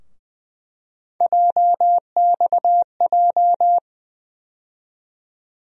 Morse code, heard as a keyed tone, about 21 wpm, 700 Hz. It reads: JXJ
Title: USA Call Sign Postfixes - 3 Letters 20wpm